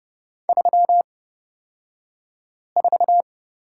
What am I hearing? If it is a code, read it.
Morse code, heard as a keyed tone, about 30 wpm, spaced as for 5 wpm, 700 Hz. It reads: 34
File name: Lesson 36d - BT - Two Characters 30wpm